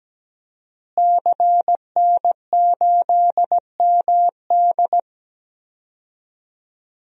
Morse code, heard as a keyed tone, about 17 wpm, 700 Hz. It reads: CN8MD
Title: DXCC-100 Call Signs - 5 Characters - Repeated 2x - 17wpm